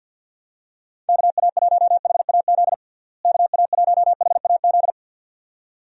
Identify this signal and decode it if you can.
Morse code, heard as a keyed tone, about 50 wpm, 700 Hz. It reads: KA1HAZ KA1HAZ
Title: USA Call Signs - 6 Characters - Repeated 2x - 50wpm